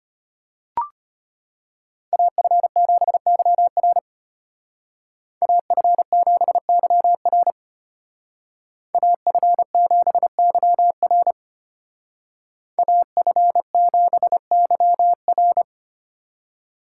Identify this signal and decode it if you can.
Morse code, heard as a keyed tone, about 32 wpm, 700 Hz. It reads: AF7YR AF7YR AF7YR AF7YR